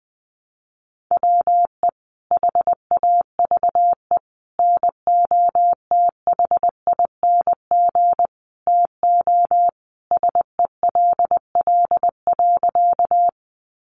Morse code, heard as a keyed tone, about 20 wpm, 700 Hz. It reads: WE HAVE NOTHING TO SELL.
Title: Sentences from Top 1100 Words 20wpm